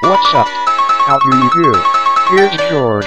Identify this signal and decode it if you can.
Speech. There is no code.